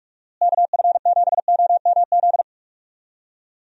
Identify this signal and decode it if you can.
Morse code, heard as a keyed tone, about 44 wpm, 700 Hz. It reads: KF7OMZ
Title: USA Call Signs - 6 Characters - Repeated 2x - 45wpm